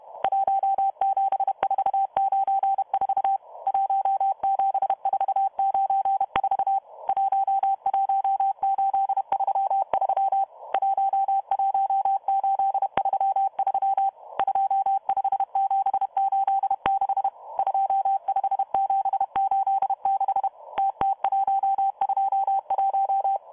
Morse code, heard as a keyed tone, about 31 wpm, 780 Hz, with radio noise. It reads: 17494 17494 11833 11833 25786 25786 TT121